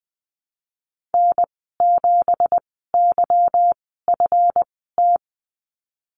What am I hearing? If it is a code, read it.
Morse code, heard as a keyed tone, about 20 wpm, spaced as for 15 wpm, 700 Hz. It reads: N7YFT
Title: 173 Lesson 28g - Number 9 - Call Signs 20wpm